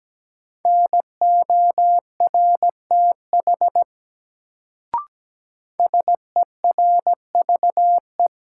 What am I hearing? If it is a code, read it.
Morse code, heard as a keyed tone, about 17 wpm, 700 Hz. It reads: NORTH SERVE
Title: Sentences from Top 400 Words - New Word Drill - 17wpm